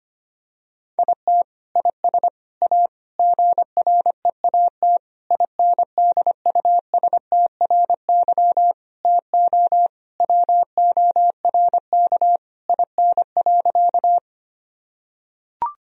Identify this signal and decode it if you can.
Morse code, heard as a keyed tone, about 25 wpm, 700 Hz. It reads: IT IS A GREAT INDUSTRY TO WORK IN.
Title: Sentences from Top 1000 Words - Review Part 4 - 25wpm